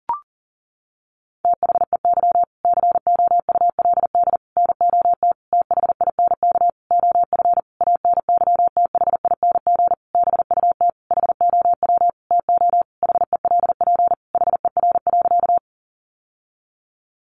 Morse code, heard as a keyed tone, about 40 wpm, 700 Hz. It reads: THEY COULD NOT THINK OF ANYTHING BUT HOW TO HELP HER.
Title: Sentences from Top 1000 Words 40wpm